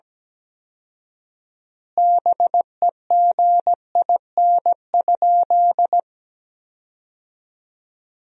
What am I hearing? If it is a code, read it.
Morse code, heard as a keyed tone, about 17 wpm, 700 Hz. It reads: BEGIN?
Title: 4x Word-Spacing - Sentences from Top 300 Words - 17wpm